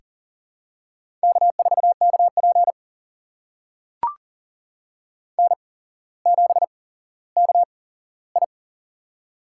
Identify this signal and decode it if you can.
Morse code, heard as a keyed tone, about 40 wpm, spaced as for 17 wpm, 700 Hz. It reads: K4KP N7KI
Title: USA Call Signs - 4 Characters - 8x Character Spacing - 40wpm